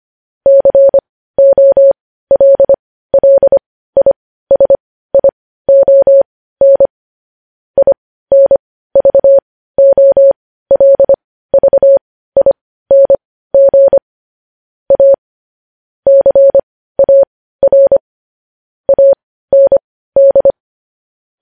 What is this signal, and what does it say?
Morse code, heard as a keyed tone, about 25 wpm, spaced as for 15 wpm, 550 Hz. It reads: COLLISION INVOLVING A CAR AND
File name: podcast / News headlines-1x-25x15